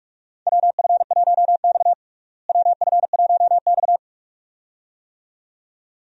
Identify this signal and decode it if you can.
Morse code, heard as a keyed tone, about 46 wpm, 700 Hz. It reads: WF1X WF1X
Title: USA Call Signs - 4 Characters - Repeated 2x - 45wpm